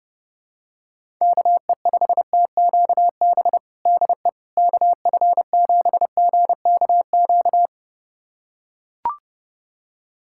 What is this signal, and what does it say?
Morse code, heard as a keyed tone, about 30 wpm, 700 Hz. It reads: KE5TQB DE KF7GKQ
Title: USA Callsign DE Callsign - 6 Characters - 30wpm